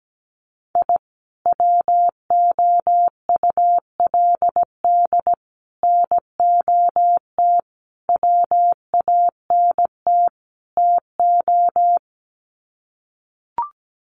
Morse code, heard as a keyed tone, about 17 wpm, 700 Hz. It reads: I WOULD NOT WANT TO
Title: Sets of 5 Words 17wpm